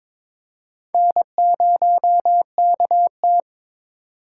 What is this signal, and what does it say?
Morse code, heard as a keyed tone, about 22 wpm, 700 Hz. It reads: N0KT
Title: USA Call Signs - 8x Character Spacing - 22wpm